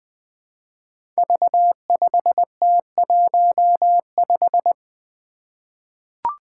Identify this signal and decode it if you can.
Morse code, heard as a keyed tone, about 20 wpm, 700 Hz. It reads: V5T15
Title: Five Challenging Letter-Number Combinations 20wpm